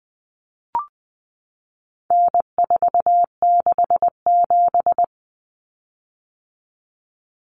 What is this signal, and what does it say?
Morse code, heard as a keyed tone, about 20 wpm, 700 Hz. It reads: N467
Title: Common Cut Numbers - 4-Digit 20wpm